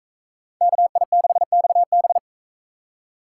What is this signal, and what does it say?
Morse code, heard as a keyed tone, about 42 wpm, 700 Hz. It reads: KI6XB